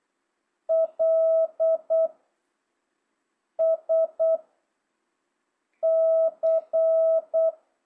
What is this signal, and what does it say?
Morse code, heard as a keyed tone, about 8 wpm, 660 Hz. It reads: L S C
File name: jh8cbh / 2025cw